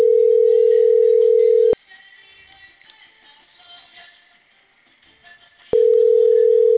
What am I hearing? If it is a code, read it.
Ringback tone (North America), 440 + 480 Hz.